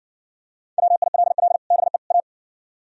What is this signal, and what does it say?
Morse code, heard as a keyed tone, about 59 wpm, 700 Hz. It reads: WILL BE A